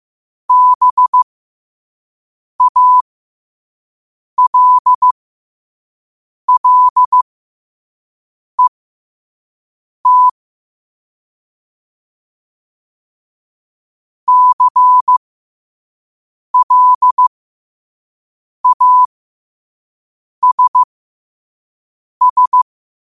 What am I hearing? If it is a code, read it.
Morse code, heard as a keyed tone, about 15 wpm, spaced as for 5 wpm, 1000 Hz. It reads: BALLET CLASS
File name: qotdfiles / 5wpm